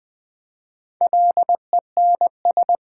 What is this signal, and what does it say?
Morse code, heard as a keyed tone, about 20 wpm, 700 Hz. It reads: LENS
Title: Easy Words with Hints - 20wpm